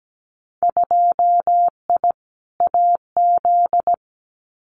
Morse code, heard as a keyed tone, about 17 wpm, 700 Hz. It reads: 2I AZ
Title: Winter Field Day Exchanges 17wpm